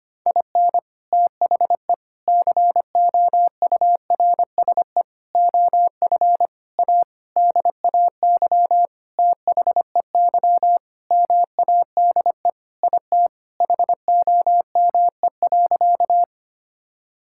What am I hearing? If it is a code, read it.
Morse code, heard as a keyed tone, about 25 wpm, 700 Hz. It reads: IN THE COURSE OF A DAY THEY MADE IT HOME.
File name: practice / 25wpm / Sentences with 4-Word Phrases - 25wpm